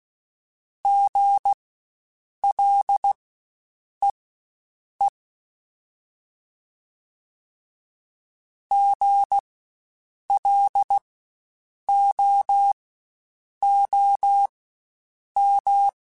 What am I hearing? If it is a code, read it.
Morse code, heard as a keyed tone, about 16 wpm, spaced as for 7 wpm, 800 Hz. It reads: GLEE GLOOM
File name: Spaced Alphas / Spaced Alphas mp3s / Gs sp @15